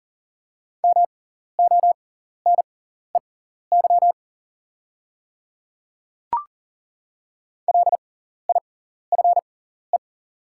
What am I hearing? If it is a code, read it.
Morse code, heard as a keyed tone, about 40 wpm, spaced as for 14 wpm, 700 Hz. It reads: MONEY LIFE
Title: 6x Character Spacing - Top 500 Words - 40wpm